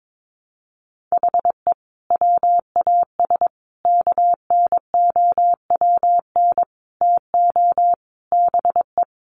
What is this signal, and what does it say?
Morse code, heard as a keyed tone, about 22 wpm, 700 Hz. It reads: HE WAS KNOWN TO BE
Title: Sentences from Top 1200 Words 22wpm